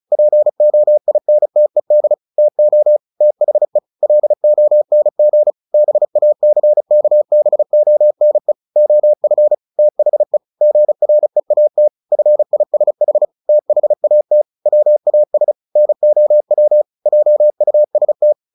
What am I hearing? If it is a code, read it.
Morse code, heard as a keyed tone, about 34 wpm, 600 Hz. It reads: POINTED TO THE LONG BACKBONE OF THE GREAT FISH THAT WAS NOW JUST
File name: the old man and the sea 2 35wpm